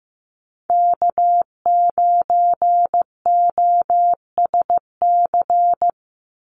Morse code, heard as a keyed tone, about 15 wpm, 700 Hz. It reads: K9OSC
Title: NA SOTA Chaser Callsigns 15wpm